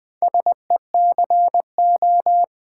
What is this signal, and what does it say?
Morse code, heard as a keyed tone, about 20 wpm, 700 Hz. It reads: SECO
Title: Build A Word - 2 Letters At A Time - 8-Letter Words - 20wpm